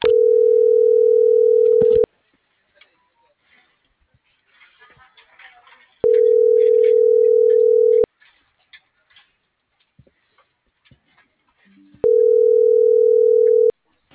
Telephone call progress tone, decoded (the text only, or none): ringback tone